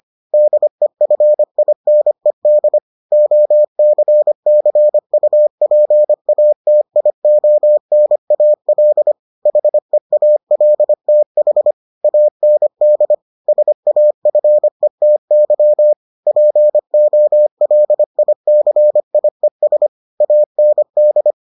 Morse code, heard as a keyed tone, about 25 wpm, 600 Hz. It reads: DEFINED OCCUPATIONAL HEALTH AND SAFETY POLICIES AND